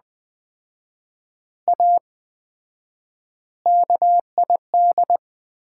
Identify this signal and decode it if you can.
Morse code, heard as a keyed tone, about 20 wpm, 700 Hz. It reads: A KID
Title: 4x Word-Spacing - Sets of 5 Words - 20wpm